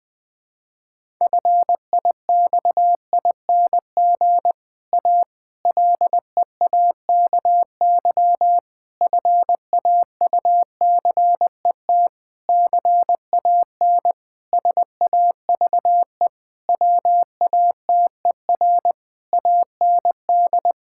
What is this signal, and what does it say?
Morse code, heard as a keyed tone, about 21 wpm, 700 Hz. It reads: FIXING A LEAKY FAUCET CAN SAVE WATER AND
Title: Difficult Sentences with Hints - 20wpm